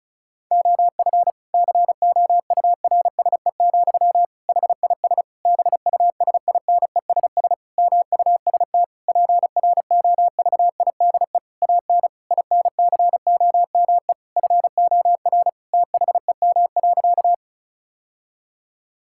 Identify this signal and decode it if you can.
Morse code, heard as a keyed tone, about 36 wpm, 700 Hz. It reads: OF COURSE, HIS BUSINESS MUST PROVIDE AN INCOME FOR THEM.
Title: Sentences from Top 1000 Words - Review Part 5 - 35wpm